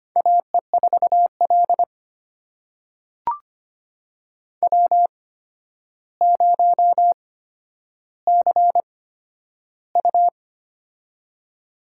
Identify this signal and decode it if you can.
Morse code, heard as a keyed tone, about 25 wpm, spaced as for 11 wpm, 700 Hz. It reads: AE4L W0CU
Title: USA Call Signs - 4 Characters - 8x Character Spacing - 25wpm